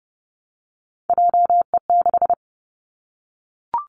Morse code, heard as a keyed tone, about 30 wpm, 700 Hz. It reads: JE6
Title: Three Challenging Letter-Number Combinations 30wpm